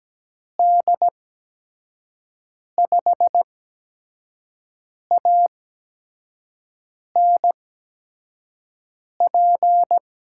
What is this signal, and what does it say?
Morse code, heard as a keyed tone, about 17 wpm, spaced as for 5 wpm, 700 Hz. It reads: D5ANP
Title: USA Call Signs - 6 Characters - 8x Character Spacing - 17wpm